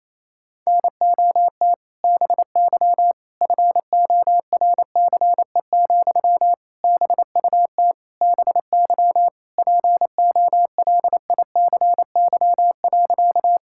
Morse code, heard as a keyed tone, about 28 wpm, 700 Hz. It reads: NOT BY FORCE, BUT BY POLICY.
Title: Sentences from Top 1000 Words - Review Part 5 - 28wpm